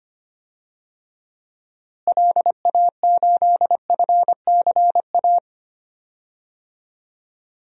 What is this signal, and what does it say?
Morse code, heard as a keyed tone, about 25 wpm, 700 Hz. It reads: LA8FCA
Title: DXCC-100 Call Signs - 6 Characters - Repeated 2x - 25wpm